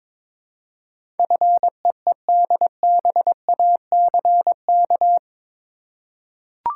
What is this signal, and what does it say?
Morse code, heard as a keyed tone, about 22 wpm, 700 Hz. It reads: FEEDBACK
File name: practice / 22wpm / Eight Letter Words - 22wpm